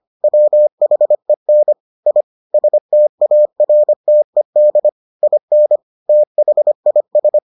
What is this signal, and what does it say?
Morse code, heard as a keyed tone, about 25 wpm, 600 Hz. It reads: WHEN I STARTED IN THIS